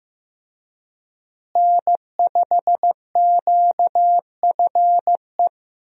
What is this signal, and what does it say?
Morse code, heard as a keyed tone, about 14 wpm, 700 Hz. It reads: N5QFE
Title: USA Call Signs - 5 Characters - Repeated 3x - 15wpm